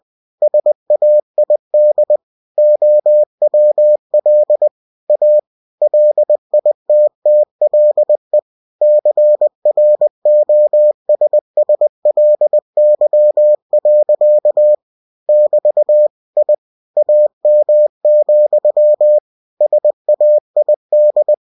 Morse code, heard as a keyed tone, about 20 wpm, 600 Hz. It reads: SAID OWL A LITTLE CROSSLY. <BT> I AM, SAID
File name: winnie-20wpm